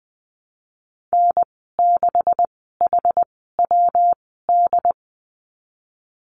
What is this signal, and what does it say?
Morse code, heard as a keyed tone, about 20 wpm, spaced as for 14 wpm, 700 Hz. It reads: N6HWD
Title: 147 Lesson 24g - Letter y - Call Signs 20wpm